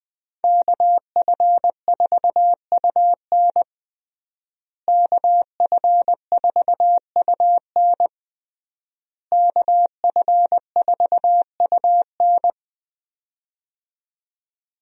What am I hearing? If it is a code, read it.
Morse code, heard as a keyed tone, about 21 wpm, 700 Hz. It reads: KF4UN KF4UN KF4UN
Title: USA Call Signs - 5 Characters - Repeated 3x - 20wpm